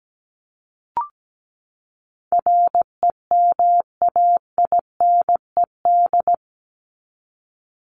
Morse code, heard as a keyed tone, about 17 wpm, 700 Hz. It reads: REMAINED
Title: ED Suffix - Top 100 Words with Base Words 17wpm